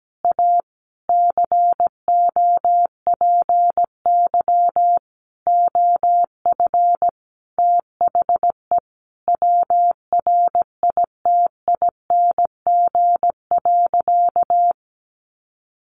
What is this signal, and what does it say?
Morse code, heard as a keyed tone, about 17 wpm, 700 Hz. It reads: A COPY OF THE WRITING.
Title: Sentences from Top 1400 Words 17wpm